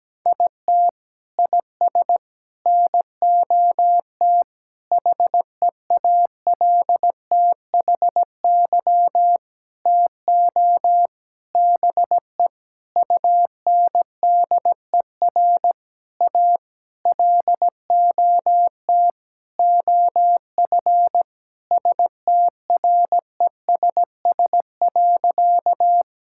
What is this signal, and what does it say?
Morse code, heard as a keyed tone, about 17 wpm, 700 Hz. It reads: IT IS NOT HEALTHY TO BE UNDER A LOT OF STRESS.
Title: Sentences from Top 1000 Words - Review Part 6 - 17wpm